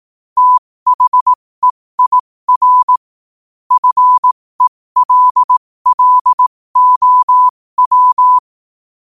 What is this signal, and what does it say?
Morse code, heard as a keyed tone, about 18 wpm, spaced as for 15 wpm, 1000 Hz. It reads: THEIR FELLOW
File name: qotdfiles / 15wpm